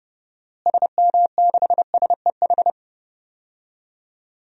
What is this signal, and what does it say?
Morse code, heard as a keyed tone, about 30 wpm, 700 Hz. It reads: SM6SEH
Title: DXCC-100 Call Signs - 6 Characters - Repeated 2x - 30wpm